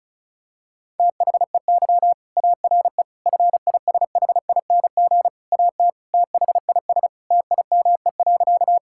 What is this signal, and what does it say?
Morse code, heard as a keyed tone, about 36 wpm, 700 Hz. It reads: THEY ARE FISHING AT THIS TIME.
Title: Sentences from Top 1000 Words - Review Part 5 - 35wpm